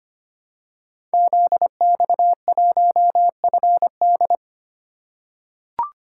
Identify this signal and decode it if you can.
Morse code, heard as a keyed tone, about 25 wpm, 700 Hz. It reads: ZX1FD
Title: Code Groups - 5 Letter:Numbers - 25wpm